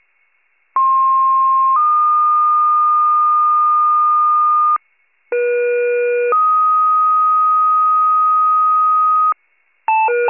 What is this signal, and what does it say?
Two-tone page, heard as a one-second tone, then a three-second tone, about 1060 then 1220 Hz, about 480 then 1250 Hz. It is heard from 0.8 to 9.3 s.
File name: wcalert